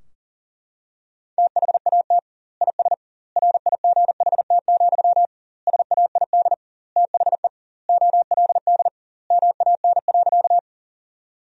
Morse code, heard as a keyed tone, about 40 wpm, 700 Hz. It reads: THAT IS RIGHT, SAID THE OLD MAN.
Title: 2x Word-Spacing - Sentences from Top 200 Words - 40wpm